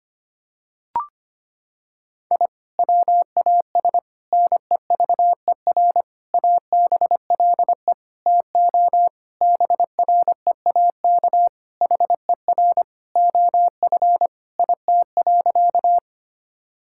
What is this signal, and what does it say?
Morse code, heard as a keyed tone, about 25 wpm, 700 Hz. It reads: I WAS NEVER ABLE TO BREAK HER OF IT.
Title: Sentences from Top 1000 Words - Review Part 6 - 25wpm